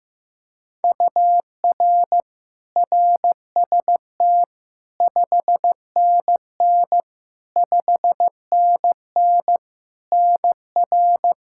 Morse code, heard as a keyed tone, about 15 wpm, 700 Hz. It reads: UR RST 5NN 5NN NR
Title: Practice QSO - Exchange 1 - Beginning - 15wpm